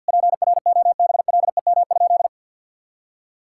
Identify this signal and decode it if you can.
Morse code, heard as a keyed tone, about 50 wpm, 700 Hz. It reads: PROBLEM?
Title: Sentences from Top 1000 Words - Review Part 6 - 50wpm